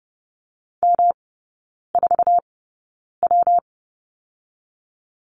Morse code, heard as a keyed tone, about 30 wpm, 700 Hz. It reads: M 4 W
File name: Lesson 19e - Letter w - Three Characters 30wpm